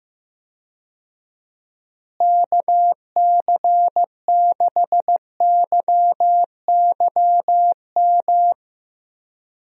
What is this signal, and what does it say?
Morse code, heard as a keyed tone, about 15 wpm, 700 Hz. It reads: KC6YYM
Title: USA Call Signs - 6 Characters - Repeated 2x - 15wpm